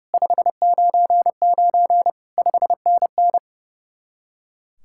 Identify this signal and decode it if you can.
Morse code, heard as a keyed tone, about 38 wpm, 700 Hz. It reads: 599 5NN